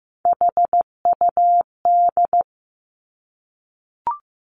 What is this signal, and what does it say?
Morse code, heard as a keyed tone, about 15 wpm, 700 Hz. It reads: HUD
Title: Three Challenging Letter-Number Combinations 15wpm